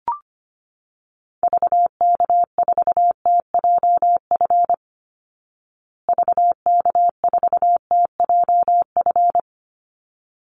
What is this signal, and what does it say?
Morse code, heard as a keyed tone, about 25 wpm, 700 Hz. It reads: VK4TJF VK4TJF
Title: Call Signs Repeated 3x - 25wpm